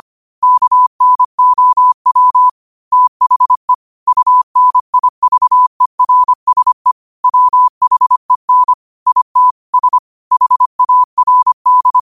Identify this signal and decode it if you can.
Morse code, heard as a keyed tone, about 25 wpm, 1000 Hz. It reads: KNOW THE UNIVERSE WHEN IT S HARD
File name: qotdfiles / 25wpm